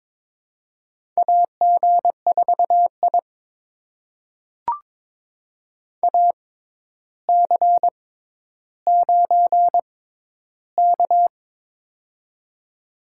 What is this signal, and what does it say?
Morse code, heard as a keyed tone, about 22 wpm, spaced as for 11 wpm, 700 Hz. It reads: AG4I AC9K
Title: USA Call Signs - 4 Characters - 6x Character Spacing - 22wpm